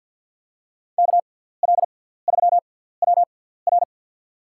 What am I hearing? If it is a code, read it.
Morse code, heard as a keyed tone, about 52 wpm, spaced as for 17 wpm, 700 Hz. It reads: KL3WR
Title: USA Call Signs - 5 Characters - 6x Character Spacing - 50wpm